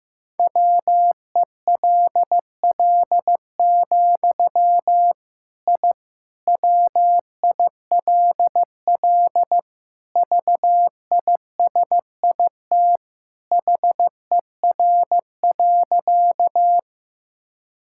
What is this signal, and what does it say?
Morse code, heard as a keyed tone, about 16 wpm, 700 Hz. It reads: WELL, I WILL VISIT HER.
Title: Sentences from Top 1000 Words - Review Part 6 - 15wpm